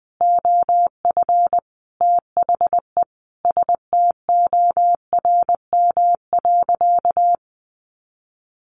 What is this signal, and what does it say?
Morse code, heard as a keyed tone, about 20 wpm, 700 Hz. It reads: OF THE STORM.
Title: Sentences from Top 1500 Words 20wpm